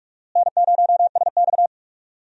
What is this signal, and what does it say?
Morse code, heard as a keyed tone, about 41 wpm, 700 Hz. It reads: N0SX
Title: USA Call Signs - 4 Characters - Repeated 3x - 45wpm